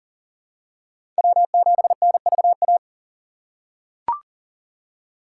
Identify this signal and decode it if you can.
Morse code, heard as a keyed tone, about 40 wpm, 700 Hz. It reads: W7NVA
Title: USA Call Signs - 5 Characters - 4x Character Spacing - 40wpm